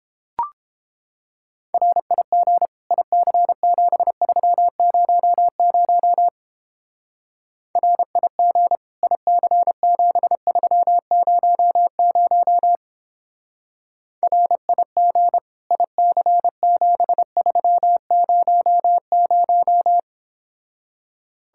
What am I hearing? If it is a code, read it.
Morse code, heard as a keyed tone, about 29 wpm, 700 Hz. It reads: RIG IC7300 RIG IC7300 RIG IC7300